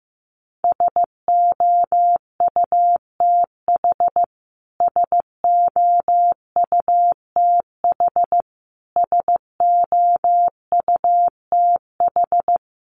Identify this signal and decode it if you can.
Morse code, heard as a keyed tone, about 15 wpm, 700 Hz. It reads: SOUTH SOUTH SOUTH
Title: Top 500 Words Repeated 15wpm